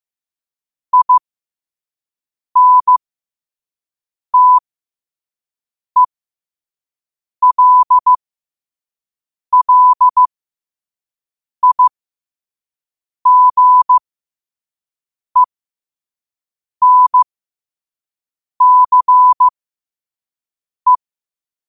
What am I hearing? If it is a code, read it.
Morse code, heard as a keyed tone, about 15 wpm, spaced as for 5 wpm, 1000 Hz. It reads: INTELLIGENCE